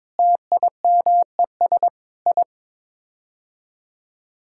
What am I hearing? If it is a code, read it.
Morse code, heard as a keyed tone, about 22 wpm, 700 Hz. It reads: TIMES I
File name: practice / 22wpm / Sets of 8 Words - 22wpm